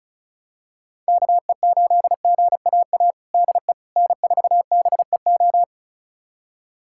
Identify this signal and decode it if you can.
Morse code, heard as a keyed tone, about 36 wpm, 700 Hz. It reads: KE8GAA DE N4BEO
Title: USA Callsign DE Callsign 35wpm